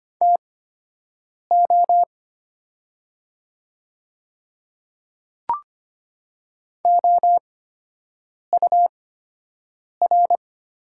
Morse code, heard as a keyed tone, about 25 wpm, spaced as for 7 wpm, 700 Hz. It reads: TO OUR
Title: 8x Character Spacing - Top 200 Words - 25wpm